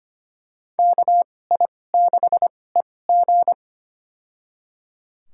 Morse code, heard as a keyed tone, about 25 wpm, spaced as for 18 wpm, 700 Hz. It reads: KI6EG
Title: USA Call Signs - 2x Character Spacing - 25wpm